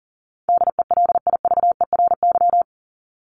Morse code, heard as a keyed tone, about 40 wpm, 700 Hz. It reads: DELIVERY
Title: Top 1251-1500 Words 40wpm